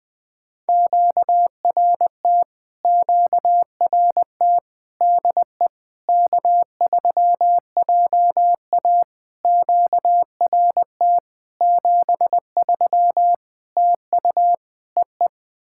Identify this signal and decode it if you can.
Morse code, heard as a keyed tone, about 21 wpm, 700 Hz. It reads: QRT QRT DE K3JA QRT 73 TU EE
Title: POTA QSO Practice - 20wpm